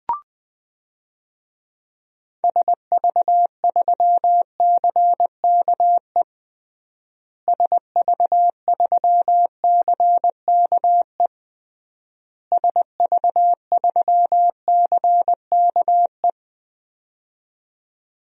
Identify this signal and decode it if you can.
Morse code, heard as a keyed tone, about 20 wpm, 700 Hz. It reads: SV3CKE SV3CKE SV3CKE
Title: DXCC-100 Call Signs - 6 Characters - Repeated 3x - 20wpm